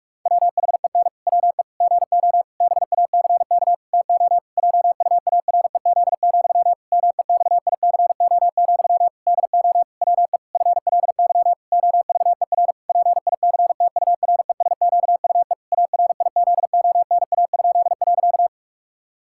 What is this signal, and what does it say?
Morse code, heard as a keyed tone, about 46 wpm, 700 Hz. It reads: WHEN WE GO BACK TO JUAREZ, MEXICO, DO WE FLY OVER PICTURESQUE ARIZONA?.
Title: Pangram Challenge v2 - 45wpm